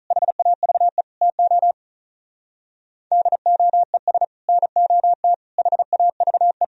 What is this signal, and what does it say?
Morse code, heard as a keyed tone, about 44 wpm, 700 Hz. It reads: HAVE TO DOES NOT HAVE